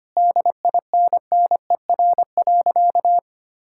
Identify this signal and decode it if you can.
Morse code, heard as a keyed tone, about 25 wpm, 700 Hz. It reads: DINNER.
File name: practice / 25wpm / Difficult Sentences with Hints - 25wpm